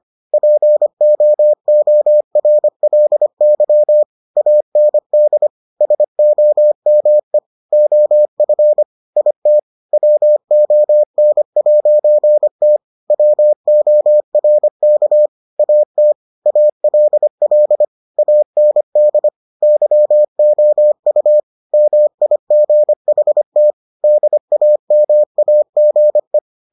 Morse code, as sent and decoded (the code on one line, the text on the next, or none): .--. --- --- .-. .-.. -.-- / .- -. -.. / ... --- -- . / --- ..-. / .. - / .-- --- -. .----. - / .-- --- .-. -.- / .- - / .- .-.. .-.. / .- -. -.. / -.-- --- ..- / -- .. --. .... - / -.. .- -- .- --. .
POORLY AND SOME OF IT WON'T WORK AT ALL AND YOU MIGHT DAMAGE